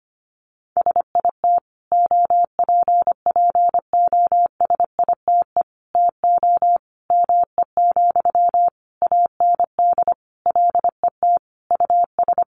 Morse code, heard as a keyed tone, about 25 wpm, 700 Hz. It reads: SIT OPPOSITE TO ME, AND LET US
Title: Sentences from Top 1300 Words 25wpm